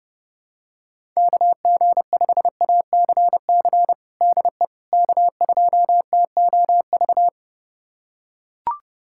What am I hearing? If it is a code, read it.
Morse code, heard as a keyed tone, about 30 wpm, 700 Hz. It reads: KG5ACC DE K2TOV
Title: USA Callsign DE Callsign 30wpm